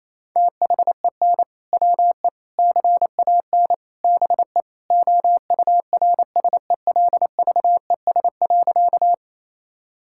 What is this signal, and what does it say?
Morse code, heard as a keyed tone, about 28 wpm, 700 Hz. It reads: THEN WE CAN BE OURSELVES.
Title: Sentences from Top 1600 Words 28wpm